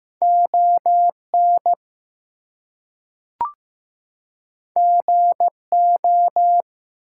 Morse code, heard as a keyed tone, about 15 wpm, 700 Hz. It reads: ON GO
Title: Two Letter Words - 15wpm